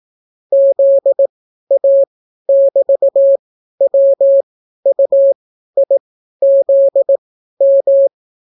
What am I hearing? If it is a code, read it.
Morse code, heard as a keyed tone, about 18 wpm, spaced as for 12 wpm, 550 Hz. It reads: ZA<BT>WUIZM